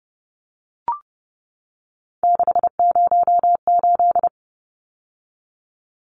Morse code, heard as a keyed tone, about 30 wpm, 700 Hz. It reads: 608
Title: Numbers - 3-Digit 30wpm